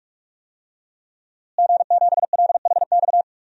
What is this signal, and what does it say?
Morse code, heard as a keyed tone, about 44 wpm, 700 Hz. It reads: G7LHX
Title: DXCC-100 Call Signs - 5 Characters - Repeated 3x - 45wpm